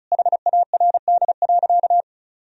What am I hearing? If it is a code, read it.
Morse code, heard as a keyed tone, about 36 wpm, 700 Hz. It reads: HARD.
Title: Sentences from Top 1000 Words - Review Part 4 - 35wpm